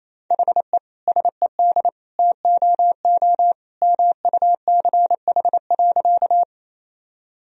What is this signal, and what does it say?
Morse code, heard as a keyed tone, about 27 wpm, 700 Hz. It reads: HE SED TOO MUCH.
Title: Sentences from Top 200 Words and Abbreviations 28wpm